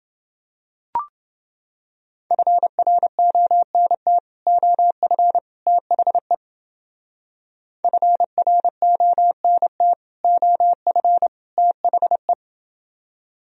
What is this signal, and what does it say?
Morse code, heard as a keyed tone, about 29 wpm, 700 Hz. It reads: FRONT OF THE FRONT OF THE